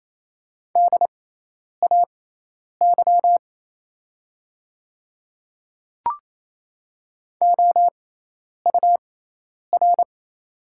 Morse code, heard as a keyed tone, about 28 wpm, spaced as for 10 wpm, 700 Hz. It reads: DAY OUR
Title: 6x Character Spacing - Top 200 Words - 28wpm